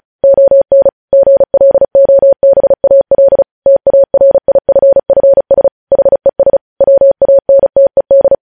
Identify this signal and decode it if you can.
Morse code, heard as a keyed tone, about 36 wpm, 550 Hz. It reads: ON GLOBAL TARIFFS HES WANTED
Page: news headlines in morse code at 35 wpm